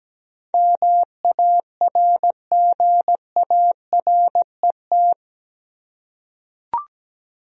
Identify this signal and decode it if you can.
Morse code, heard as a keyed tone, about 17 wpm, 700 Hz. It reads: MARGARET
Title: USA Common Names - 17wpm